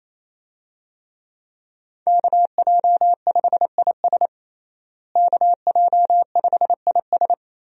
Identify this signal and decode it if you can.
Morse code, heard as a keyed tone, about 28 wpm, 700 Hz. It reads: KJ5IS KJ5IS
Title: USA Call Signs - 5 Characters - Repeated 2x - 28wpm